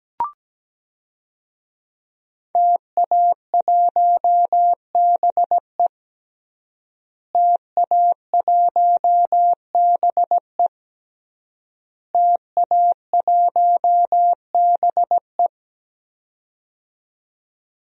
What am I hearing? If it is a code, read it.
Morse code, heard as a keyed tone, about 17 wpm, 700 Hz. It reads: TA1BE TA1BE TA1BE
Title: DXCC-100 Call Signs - 5 Characters - Repeated 3x - 17wpm